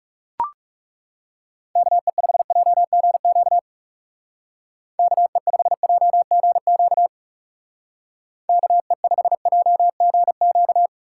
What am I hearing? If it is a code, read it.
Morse code, heard as a keyed tone, about 36 wpm, 700 Hz. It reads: KE5JGQ KE5JGQ KE5JGQ
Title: Speed Racing - USA 2x3 Call Signs 30wpm